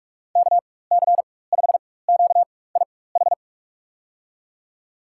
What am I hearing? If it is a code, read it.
Morse code, heard as a keyed tone, about 42 wpm, spaced as for 21 wpm, 700 Hz. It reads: KC5QIH
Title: USA Call Signs - 6 Characters - 4x Character Spacing - 45wpm